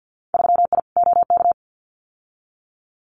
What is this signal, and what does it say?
Morse code, heard as a keyed tone, about 50 wpm, 700 Hz. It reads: 3I OK
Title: Winter Field Day Exchanges Encore 50wpm